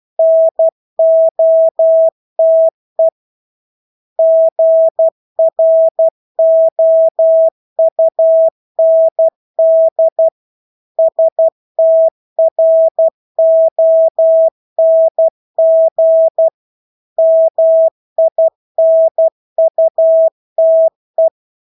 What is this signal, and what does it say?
Morse code, heard as a keyed tone, about 12 wpm, 650 Hz. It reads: NOTE GROUND STRONG MINUTE
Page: Engelske ord 12 wpm